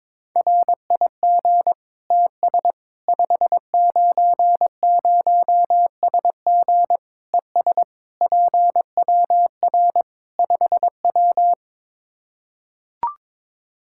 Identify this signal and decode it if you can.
Morse code, heard as a keyed tone, about 22 wpm, 700 Hz. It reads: RIG TS 590SG ES PWR 5W
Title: Practice QSO - Exchange 2 - Info - 22wpm